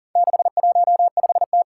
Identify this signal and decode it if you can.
Morse code, heard as a keyed tone, about 40 wpm, 700 Hz. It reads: 615T
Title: Four Challenging Letter-Number Combinations 40wpm